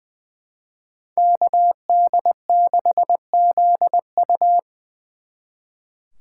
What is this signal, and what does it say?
Morse code, heard as a keyed tone, about 25 wpm, 700 Hz. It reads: KD6ZU
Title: Speed Racing - USA 2x2 Call Signs 20wpm